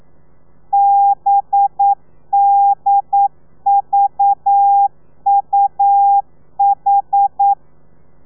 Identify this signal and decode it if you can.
Morse code, heard as a keyed tone, about 9 wpm, 800 Hz. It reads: BDVUH
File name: telegrafia / i2tfj / Lez4 40